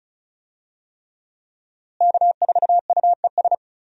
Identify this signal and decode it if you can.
Morse code, heard as a keyed tone, about 36 wpm, 700 Hz. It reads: K4UES
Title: USA Call Signs - 5 Characters - Repeated 3x - 35wpm